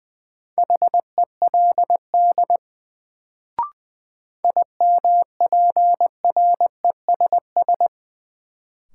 Morse code, heard as a keyed tone, about 21 wpm, 700 Hz. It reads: HELD IMPRESS
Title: Sentences from Top 1600 Words - New Word Drill - 20wpm